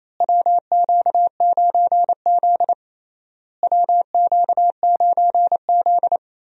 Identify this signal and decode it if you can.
Morse code, heard as a keyed tone, about 28 wpm, 700 Hz. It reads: WQ9Z WQ9Z
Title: USA Call Signs - 4 Characters - Repeated 3x - 28wpm